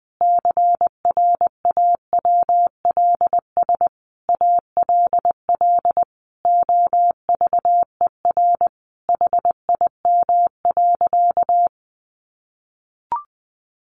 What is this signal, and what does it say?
Morse code, heard as a keyed tone, about 20 wpm, 700 Hz. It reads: CRAWLS ALL OVER HIM.
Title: Additional Easy Phrases 20wpm